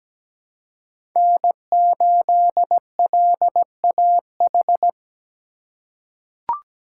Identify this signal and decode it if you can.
Morse code, heard as a keyed tone, about 17 wpm, 700 Hz. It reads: N8LAH
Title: USA Call Signs - 5 Characters - 2x Character Spacing - 17wpm